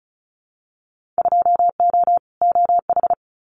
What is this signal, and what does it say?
Morse code, heard as a keyed tone, about 36 wpm, 700 Hz. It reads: 2O OH
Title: Winter Field Day Exchanges 35wpm